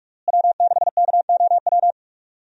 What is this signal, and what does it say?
Morse code, heard as a keyed tone, about 46 wpm, 700 Hz. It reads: W6KOW
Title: USA Call Signs - 5 Characters - Repeated 2x - 45wpm